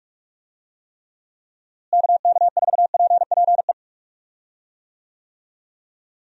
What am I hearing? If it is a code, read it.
Morse code, heard as a keyed tone, about 42 wpm, 700 Hz. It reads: KK4PPE
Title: USA Call Signs - 6 Characters - Repeated 3x - 45wpm